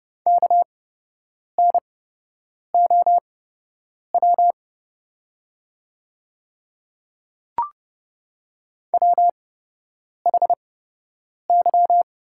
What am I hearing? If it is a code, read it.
Morse code, heard as a keyed tone, about 30 wpm, spaced as for 8 wpm, 700 Hz. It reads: KNOW WHY
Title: 8x Character Spacing - Top 200 Words - 30wpm